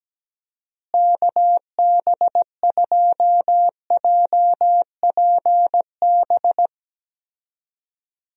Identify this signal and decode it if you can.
Morse code, heard as a keyed tone, about 17 wpm, 700 Hz. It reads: KB2JPB
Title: USA Call Signs - 6 Characters - Repeated 3x - 17wpm